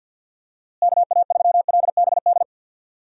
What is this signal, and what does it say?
Morse code, heard as a keyed tone, about 50 wpm, 700 Hz. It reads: KA3LBD
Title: USA Call Signs - 6 Characters - Repeated 2x - 50wpm